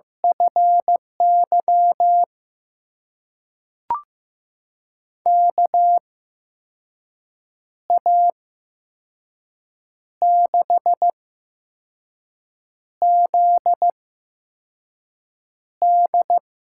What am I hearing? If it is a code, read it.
Morse code, heard as a keyed tone, about 15 wpm, spaced as for 5 wpm, 700 Hz. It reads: FY KA6ZD
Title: USA Call Signs - 6 Characters - 8x Character Spacing - 15wpm